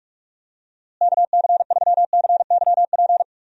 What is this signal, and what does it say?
Morse code, heard as a keyed tone, about 45 wpm, 700 Hz. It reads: KC3CYP
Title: USA Call Signs - 6 Characters - Repeated 3x - 45wpm